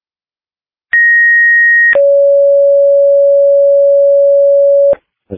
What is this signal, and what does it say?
Two-tone page, heard as a one-second tone, then a three-second tone, about 1900 then 560 Hz. It is heard from 0.9 to 4.9 s.